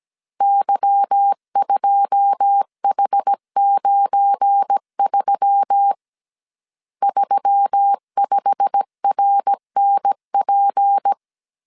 Morse code, heard as a keyed tone, about 17 wpm, 800 Hz. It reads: Y2H93 35RNP
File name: CW / DET / 1 9 1500C S A 16wpm 2828